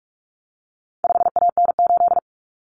Morse code, heard as a keyed tone, about 44 wpm, 700 Hz. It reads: 5AN8
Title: Common Cut Numbers - 4-Digit 45wpm